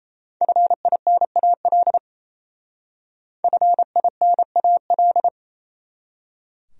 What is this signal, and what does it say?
Morse code, heard as a keyed tone, about 35 wpm, 700 Hz. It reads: FINAL FINAL
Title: Speed-Racing Top 500 Words 28wpm